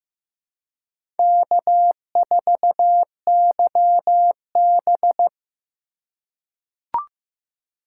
Morse code, heard as a keyed tone, about 15 wpm, 700 Hz. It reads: K4YB
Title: USA Call Signs - 4 Characters - 2x Character Spacing - 15wpm